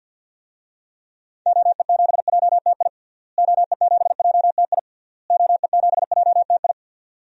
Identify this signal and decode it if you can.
Morse code, heard as a keyed tone, about 52 wpm, 700 Hz. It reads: OE7JTI OE7JTI OE7JTI
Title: DXCC-100 Call Signs - 6 Characters - Repeated 3x - 50wpm